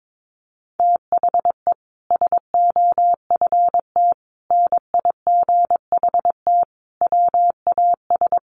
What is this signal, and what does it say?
Morse code, heard as a keyed tone, about 22 wpm, 700 Hz. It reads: THE SOFT NIGHT WAS
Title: Sentences from Top 1500 Words 22wpm